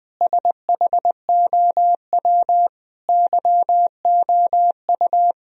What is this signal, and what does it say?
Morse code, heard as a keyed tone, about 20 wpm, 700 Hz. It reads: SHOW YOU
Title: Sentences from Top 1000 Words - Review Part 3 - 20wpm